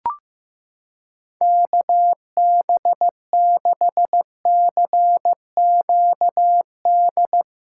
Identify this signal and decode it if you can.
Morse code, heard as a keyed tone, about 15 wpm, 700 Hz. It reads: KB6CQD
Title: USA Callsign DE Callsign 15wpm